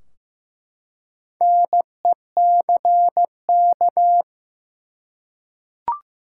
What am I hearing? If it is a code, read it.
Morse code, heard as a keyed tone, about 15 wpm, 700 Hz. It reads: NECK
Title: Four Letter Words - 15wpm